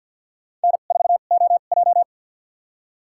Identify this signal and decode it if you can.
Morse code, heard as a keyed tone, about 51 wpm, spaced as for 36 wpm, 700 Hz. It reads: N4OJ
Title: USA Call Signs - 4 Characters - 2x Character Spacing - 50wpm